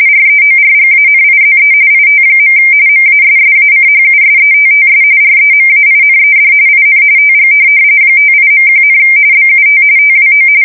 RTTY, about 45.45 baud. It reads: REORGANIZE IT. MASSACHUSETTS REMAINED IN TG 58.1, BUT TWO
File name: fsk170 / USS-Massachusetts